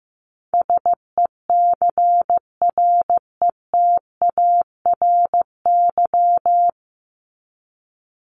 Morse code, heard as a keyed tone, about 15 wpm, 700 Hz. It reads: SECRETARY
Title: Top 1251-1500 Words 15wpm